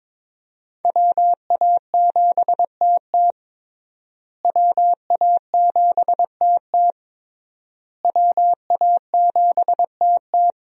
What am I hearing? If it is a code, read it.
Morse code, heard as a keyed tone, about 22 wpm, 700 Hz. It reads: WA7TT WA7TT WA7TT
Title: USA Call Signs - 5 Characters - Repeated 3x - 22wpm